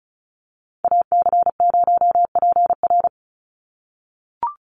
Morse code, heard as a keyed tone, about 34 wpm, 700 Hz. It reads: AC0PR
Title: NA SOTA Activator Callsigns 35wpm